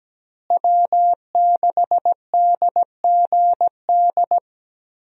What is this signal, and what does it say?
Morse code, heard as a keyed tone, about 17 wpm, 700 Hz. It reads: W6DGD
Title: USA Call Signs - 5 Characters - Repeated 3x - 17wpm